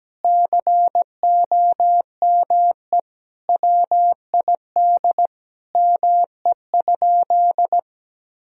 Morse code, heard as a keyed tone, about 17 wpm, 700 Hz. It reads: COME WID ME?
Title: Sentences from Top 200 Words and Abbreviations 17wpm